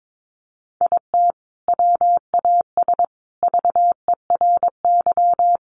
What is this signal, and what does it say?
Morse code, heard as a keyed tone, about 22 wpm, 700 Hz. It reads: IT WAS VERY
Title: Sentences from Top 1100 Words 22wpm